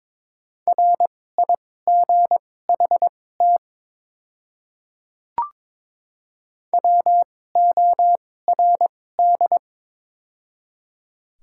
Morse code, heard as a keyed tone, about 22 wpm, spaced as for 16 wpm, 700 Hz. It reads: RIGHT WORD
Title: 2x Character Spacing - Top 200 Words - 22wpm